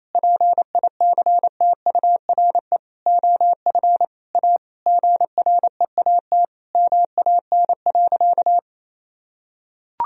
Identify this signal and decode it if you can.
Morse code, heard as a keyed tone, about 28 wpm, 700 Hz. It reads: PICTURE OF A GREAT MAN.
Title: Sentences from Top 1000 Words - Review Part 2 - 28wpm